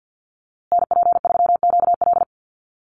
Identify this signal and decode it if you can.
Morse code, heard as a keyed tone, about 50 wpm, 700 Hz. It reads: NP3QL
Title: USA Call Signs Repeated 3x 50wpm